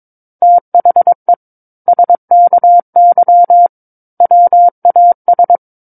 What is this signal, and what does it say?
Morse code, heard as a keyed tone, about 22 wpm, 700 Hz. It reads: THE SKY WAS
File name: Book One / 13-Chapter 13 - 22 WPM